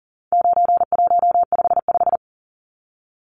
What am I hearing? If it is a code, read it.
Morse code, heard as a keyed tone, about 40 wpm, 700 Hz. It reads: 9155
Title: Numbers - 4-Digit 40wpm